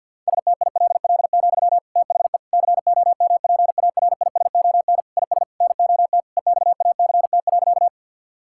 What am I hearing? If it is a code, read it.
Morse code, heard as a keyed tone, about 50 wpm, 700 Hz. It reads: STILL, THE COMPARISON IS NOT EXACT.
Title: Sentences from Top 1600 Words 50wpm